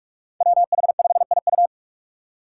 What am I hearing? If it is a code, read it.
Morse code, heard as a keyed tone, about 44 wpm, 700 Hz. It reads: WH5IU
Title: Five Challenging Letter-Number Combinations 45wpm